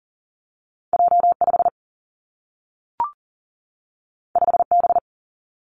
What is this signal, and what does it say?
Morse code, heard as a keyed tone, about 39 wpm, 700 Hz. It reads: J5 5B
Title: Two Challenging Letter-Number Combinations 40wpm